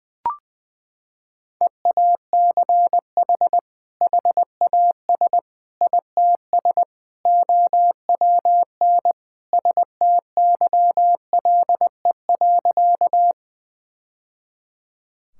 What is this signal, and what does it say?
Morse code, heard as a keyed tone, about 21 wpm, 700 Hz. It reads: EACH HAS ITS OWN STYLE.
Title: Sentences from Top 1000 Words - Review Part 5 - 20wpm